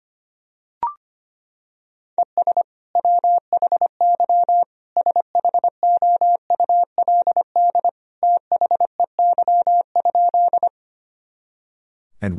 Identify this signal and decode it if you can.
Morse code, heard as a keyed tone, about 25 wpm, 700 Hz. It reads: ES WHY SHOULD THEY?
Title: Sentences from Top 300 Words and Abbreviations 25wpm